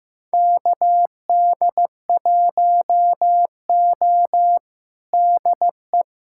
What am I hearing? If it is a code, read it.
Morse code, heard as a keyed tone, about 15 wpm, 700 Hz. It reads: KD1O DE
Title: USA Callsign DE Callsign - 4 Characters - 15wpm